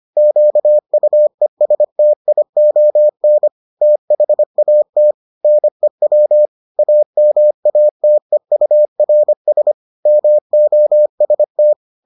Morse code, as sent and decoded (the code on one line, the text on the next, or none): --.- ..- . ... - .. --- -. / - .... .- - / -. .-- / .- -- .- - . ..- .-. ... / -- --- ... -
QUESTION THAT NEW AMATEURS MOST